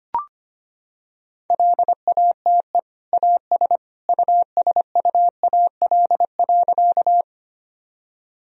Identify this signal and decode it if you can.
Morse code, heard as a keyed tone, about 25 wpm, 700 Hz. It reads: LATE AS USUAL.
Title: Sentences from Top 1000 Words - Review Part 2 - 25wpm